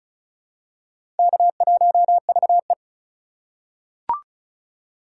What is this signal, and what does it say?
Morse code, heard as a keyed tone, about 36 wpm, 700 Hz. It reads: K1VE
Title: USA Call Signs - 4 Characters - 4x Character Spacing - 35wpm